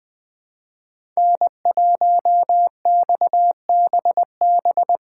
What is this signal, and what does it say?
Morse code, heard as a keyed tone, about 21 wpm, 700 Hz. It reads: N1XBB
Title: USA Call Signs - 5 Characters - 2x Character Spacing - 20wpm